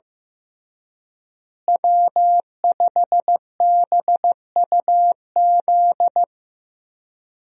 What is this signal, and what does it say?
Morse code, heard as a keyed tone, about 16 wpm, 700 Hz. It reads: W5BUZ
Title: USA Call Signs - 5 Characters - Repeated 2x - 15wpm